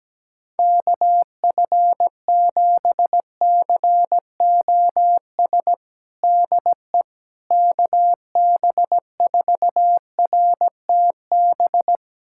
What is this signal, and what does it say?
Morse code, heard as a keyed tone, about 17 wpm, 700 Hz. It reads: KF7COS DE KB4RTB
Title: USA Callsign DE Callsign - 6 Characters - 17wpm